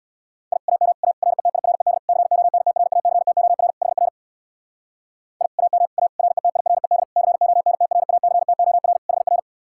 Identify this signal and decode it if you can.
Morse code, heard as a keyed tone, about 64 wpm, 700 Hz. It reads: I AM A RETIRED CONTRACTOR HR I AM A RETIRED CONTRACTOR HR